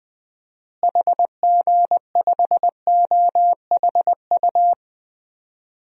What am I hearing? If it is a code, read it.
Morse code, heard as a keyed tone, about 20 wpm, 700 Hz. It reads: HG5OHU
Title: DXCC-100 Call Signs - 6 Characters - Repeated 3x - 20wpm